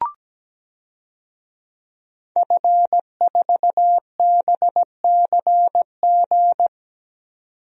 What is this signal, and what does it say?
Morse code, heard as a keyed tone, about 17 wpm, 700 Hz. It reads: F4BCG
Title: DXCC-100 Call Signs - 5 Characters - 17wpm